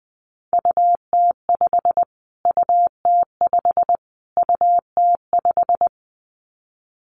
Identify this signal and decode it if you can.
Morse code, heard as a keyed tone, about 19 wpm, 700 Hz. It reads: UT5 UT5 UT5
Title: Callsign Prefixes for Top 25 DXCC - Repeated 3x - 20wpm